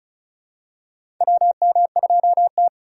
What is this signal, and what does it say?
Morse code, heard as a keyed tone, about 34 wpm, 700 Hz. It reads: WM2T
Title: USA Call Signs - 4 Characters - 6x Character Spacing - 35wpm